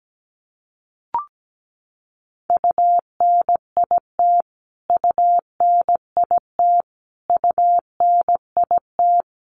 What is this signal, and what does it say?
Morse code, heard as a keyed tone, about 17 wpm, 700 Hz. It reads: UNIT UNIT UNIT
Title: Top 500 Words Repeated 17wpm